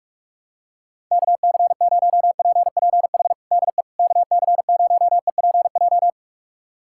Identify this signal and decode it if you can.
Morse code, heard as a keyed tone, about 46 wpm, 700 Hz. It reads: KC0PPH DE KC0EPJ
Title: USA Callsign DE Callsign 45wpm